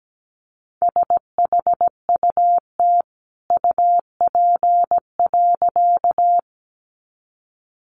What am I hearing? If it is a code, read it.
Morse code, heard as a keyed tone, about 17 wpm, 700 Hz. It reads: SHUT UP.
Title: A Sound Of Thunder 17wpm